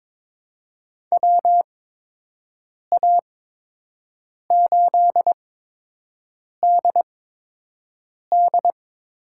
Morse code, heard as a keyed tone, about 22 wpm, spaced as for 6 wpm, 700 Hz. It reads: WA8DD
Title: USA Call Signs - 8x Character Spacing - 22wpm